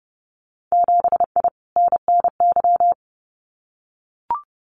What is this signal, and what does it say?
Morse code, heard as a keyed tone, about 30 wpm, 700 Hz. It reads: 7I NNY
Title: Winter Field Day Exchanges Encore 30wpm